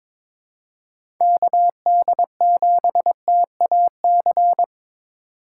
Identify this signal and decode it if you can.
Morse code, heard as a keyed tone, about 22 wpm, 700 Hz. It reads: KD7TAC
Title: USA Call Signs - 4x Character Spacing - 22wpm